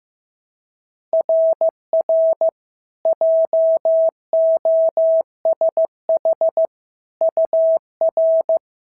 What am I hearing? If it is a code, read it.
Morse code, heard as a keyed tone, about 15 wpm, 650 Hz. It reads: RR JOSH UR
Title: SOTA QSO Practice 15wpm